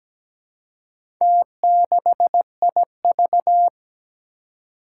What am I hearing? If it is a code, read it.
Morse code, heard as a keyed tone, about 17 wpm, 700 Hz. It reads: T6IV
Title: Four Challenging Letter-Number Combinations 17wpm